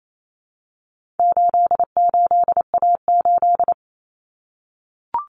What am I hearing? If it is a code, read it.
Morse code, heard as a keyed tone, about 28 wpm, 700 Hz. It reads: 88A8
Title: Common Cut Numbers - 4-Digit 28wpm